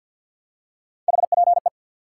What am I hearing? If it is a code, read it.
Morse code, heard as a keyed tone, about 54 wpm, 700 Hz. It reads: HPE
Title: Top 90 QSO Elements 50wpm